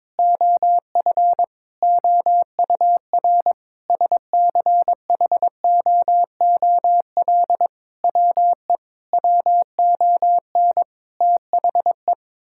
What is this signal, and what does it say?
Morse code, heard as a keyed tone, about 22 wpm, 700 Hz. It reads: OF OUR SCHOOL WE WON THE
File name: practice / 22wpm / Sentences with 8-Word Phrases - 22wpm